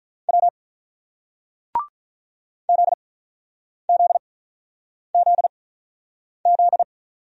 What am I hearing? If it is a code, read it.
Morse code, heard as a keyed tone, about 47 wpm, spaced as for 9 wpm, 700 Hz. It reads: W ZZZZ